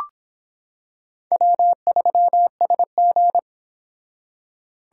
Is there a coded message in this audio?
Yes: Morse code from 1.3 to 3.4 s, heard as a keyed tone, about 26 wpm, 700 Hz. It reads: W3SG